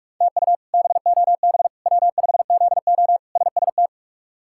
Morse code, heard as a keyed tone, about 46 wpm, 700 Hz. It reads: TU BOB W5ZO SST
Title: K1USN SST QSO Practice - 45wpm